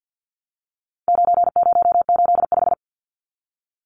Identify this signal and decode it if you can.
Morse code, heard as a keyed tone, about 50 wpm, 700 Hz. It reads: 9085
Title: Numbers - 4-Digit 50wpm